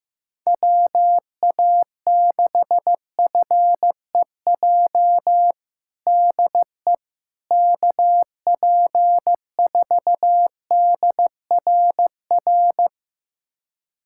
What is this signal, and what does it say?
Morse code, heard as a keyed tone, about 15 wpm, 700 Hz. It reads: WA6FEJ DE KP4DRR
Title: USA Callsign DE Callsign - 6 Characters - 15wpm